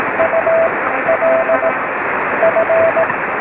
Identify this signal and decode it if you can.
Morse code, heard as a keyed tone, about 18 wpm, spaced as for 11 wpm, 650 Hz. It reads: ULF
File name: sm7lcb / oe5vrl-5 23cm